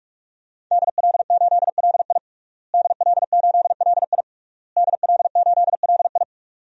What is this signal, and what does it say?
Morse code, heard as a keyed tone, about 46 wpm, 700 Hz. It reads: DL8LI DL8LI DL8LI
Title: DXCC-100 Call Signs - 5 Characters - Repeated 3x - 45wpm